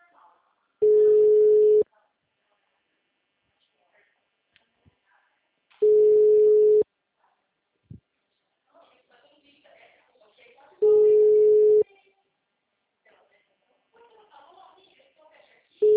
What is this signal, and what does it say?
Ringback tone (Europe, 425 Hz), with speech.